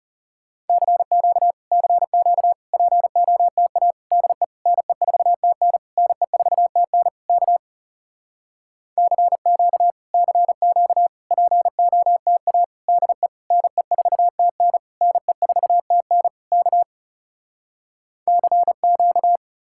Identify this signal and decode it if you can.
Morse code, heard as a keyed tone, about 40 wpm, 700 Hz. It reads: CQ CQ POTA DE NE4TN NE4TN K CQ CQ POTA DE NE4TN NE4TN K CQ